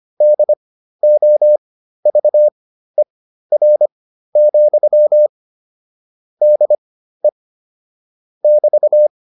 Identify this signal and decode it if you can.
Morse code, heard as a keyed tone, about 25 wpm, spaced as for 13 wpm, 600 Hz. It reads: DOVER, DE <BT>